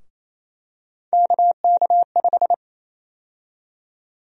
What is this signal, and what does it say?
Morse code, heard as a keyed tone, about 28 wpm, 700 Hz. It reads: KK5
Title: USA Call Sign Prefixes - Repeated 2x - 28wpm